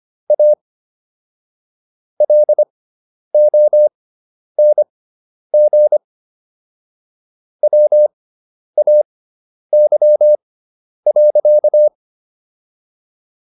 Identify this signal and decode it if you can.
Morse code, heard as a keyed tone, about 25 wpm, spaced as for 10 wpm, 600 Hz. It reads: A LONG WAY.